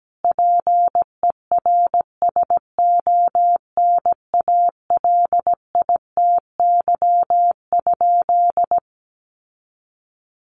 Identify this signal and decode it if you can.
Morse code, heard as a keyed tone, about 17 wpm, 700 Hz. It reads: PERSONALITY?
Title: Sentences from Top 1400 Words 17wpm